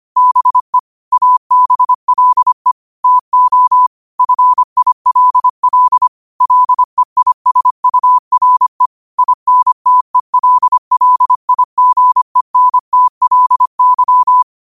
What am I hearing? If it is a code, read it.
Morse code, heard as a keyed tone, about 25 wpm, 1000 Hz. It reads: BE ABLE TO FILL LEISURE INTELLIGENTLY
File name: qotdfiles / 25wpm